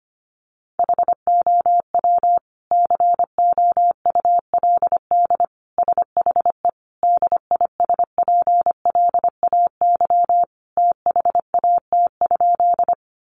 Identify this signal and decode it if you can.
Morse code, heard as a keyed tone, about 25 wpm, 700 Hz. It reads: HOW COULD SHE DISPLAY THAT?
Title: Sentences from Top 1200 Words 25wpm